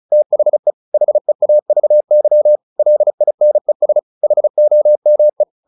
Morse code, heard as a keyed tone, about 34 wpm, 600 Hz. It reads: THE HEAVY LINES HOME
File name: the old man and the sea / the old man and the sea 1 35wpm